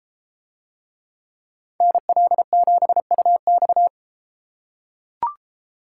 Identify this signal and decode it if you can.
Morse code, heard as a keyed tone, about 33 wpm, 700 Hz. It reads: NL7UX